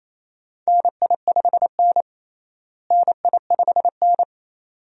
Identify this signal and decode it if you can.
Morse code, heard as a keyed tone, about 28 wpm, 700 Hz. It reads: NI5N NI5N
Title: USA Call Signs - 4 Characters - Repeated 2x - 28wpm